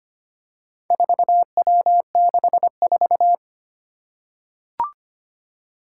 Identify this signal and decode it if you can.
Morse code, heard as a keyed tone, about 25 wpm, 700 Hz. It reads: 4W64
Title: Four Challenging Letter-Number Combinations 25wpm